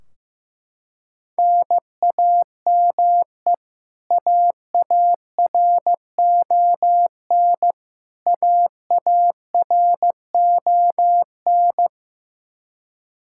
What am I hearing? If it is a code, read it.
Morse code, heard as a keyed tone, about 16 wpm, 700 Hz. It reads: NAME AARON AARON
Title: Practice QSO - Exchange 1 - Beginning - 15wpm